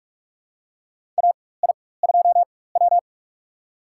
Morse code, heard as a keyed tone, about 42 wpm, spaced as for 20 wpm, 700 Hz. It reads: AI2W
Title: USA Call Signs - 4 Characters - 4x Character Spacing - 45wpm